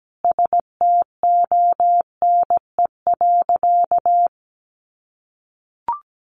Morse code, heard as a keyed tone, about 17 wpm, 700 Hz. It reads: STONE.
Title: A Sound Of Thunder 17wpm